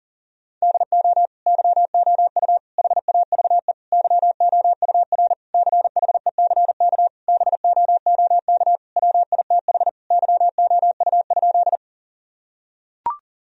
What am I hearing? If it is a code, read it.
Morse code, heard as a keyed tone, about 40 wpm, 700 Hz. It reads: DO YOU HAVE YOUR CHECK BOOK WITH YOU?
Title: Sentences from Top 1000 Words - Review Part 3 - 40wpm